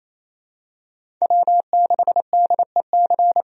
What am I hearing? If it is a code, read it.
Morse code, heard as a keyed tone, about 28 wpm, 700 Hz. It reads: W6DEC
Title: USA Call Signs - 5 Characters - 28wpm